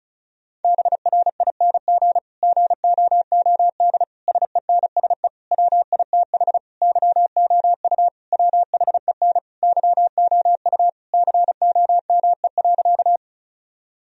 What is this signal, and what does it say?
Morse code, heard as a keyed tone, about 36 wpm, 700 Hz. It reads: BRING GOOD SENSE WITH YOU WHEN YOU COME.
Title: Sentences from Top 1000 Words - Review Part 4 - 35wpm